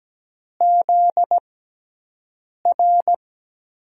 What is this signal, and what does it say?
Morse code, heard as a keyed tone, about 17 wpm, spaced as for 6 wpm, 700 Hz. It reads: ZR
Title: USA Call Signs - 5 Characters - 6x Character Spacing - 17wpm